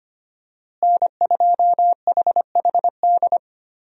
Morse code, heard as a keyed tone, about 25 wpm, 700 Hz. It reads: N2HHD
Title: USA Call Signs - 5 Characters - Repeated 2x - 25wpm